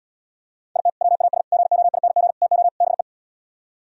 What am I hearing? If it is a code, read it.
Morse code, heard as a keyed tone, about 77 wpm, 690 Hz. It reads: IT WAS GOING TO BE